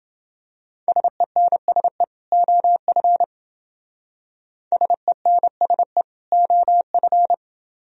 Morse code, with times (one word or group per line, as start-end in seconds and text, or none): 0.9-2.0 SENSE
2.3-3.2 OF
4.7-6.0 SENSE
6.3-7.3 OF